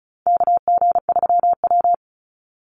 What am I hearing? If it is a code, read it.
Morse code, heard as a keyed tone, about 34 wpm, 700 Hz. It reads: KG3W
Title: NA SOTA Chaser Callsigns 35wpm